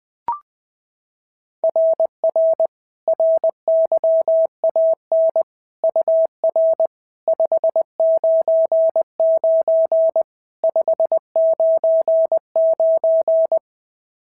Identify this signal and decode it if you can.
Morse code, heard as a keyed tone, about 20 wpm, 650 Hz. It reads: RR RYAN UR 599 599
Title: SOTA QSO Practice 20wpm